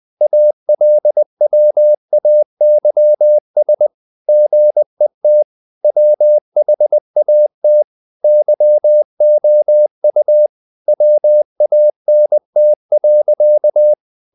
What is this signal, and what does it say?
Morse code, heard as a keyed tone, about 20 wpm, 600 Hz. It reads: ALWAYS GET WHAT YOU WANT.
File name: CW / Practice / EnglishSayings20